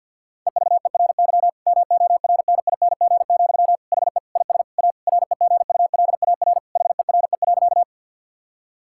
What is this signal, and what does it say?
Morse code, heard as a keyed tone, about 51 wpm, 700 Hz. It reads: EVERY MORNING, HE IS A REGULAR HERE.
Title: Sentences from Top 1000 Words - Review Part 6 - 50wpm